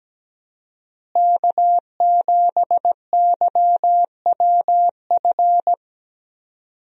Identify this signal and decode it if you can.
Morse code, heard as a keyed tone, about 17 wpm, 700 Hz. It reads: K7YWF
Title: USA Call Signs - 5 Characters - Repeated 2x - 17wpm